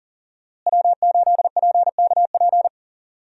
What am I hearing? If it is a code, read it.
Morse code, heard as a keyed tone, about 40 wpm, 700 Hz. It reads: W8PKP
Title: USA Call Signs - 5 Characters - Repeated 2x - 40wpm